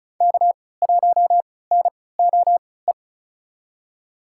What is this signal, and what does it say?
Morse code, heard as a keyed tone, about 36 wpm, spaced as for 20 wpm, 700 Hz. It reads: K1NOE
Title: USA Call Signs - 5 Characters - 3x Character Spacing - 35wpm